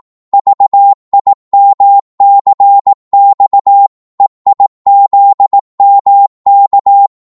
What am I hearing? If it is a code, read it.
Morse code, heard as a keyed tone, about 18 wpm, 800 Hz. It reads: VIMCX EIZMK
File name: Morse / ltr-18wpm